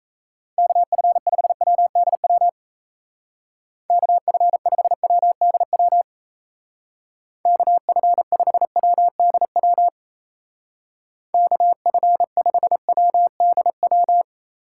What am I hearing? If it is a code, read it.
Morse code, heard as a keyed tone, about 35 wpm, 700 Hz. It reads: KF5WDW KF5WDW KF5WDW KF5WDW
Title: Speed Racing - USA 2x3 Call Signs 28wpm